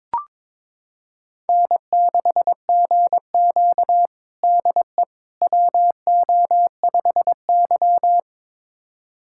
Morse code, heard as a keyed tone, about 22 wpm, 700 Hz. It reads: N6GQ DE WO5Y
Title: USA Callsign DE Callsign - 4 Characters - 22wpm